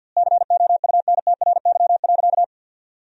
Morse code, heard as a keyed tone, about 50 wpm, 700 Hz. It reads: COUNTRY.
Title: 2x Word-Spacing - Sentences from Top 300 Words - 50wpm